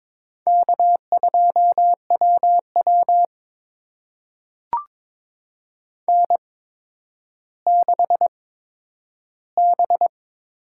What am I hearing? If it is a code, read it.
Morse code, heard as a keyed tone, about 22 wpm, spaced as for 11 wpm, 700 Hz. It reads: K2WW N6B
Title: USA Call Signs - 4 Characters - 8x Character Spacing - 22wpm